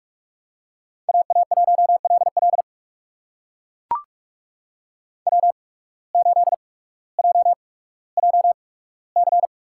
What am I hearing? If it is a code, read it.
Morse code, heard as a keyed tone, about 44 wpm, spaced as for 19 wpm, 700 Hz. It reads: AA1LL W8JJC
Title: USA Call Signs - 5 Characters - 8x Character Spacing - 45wpm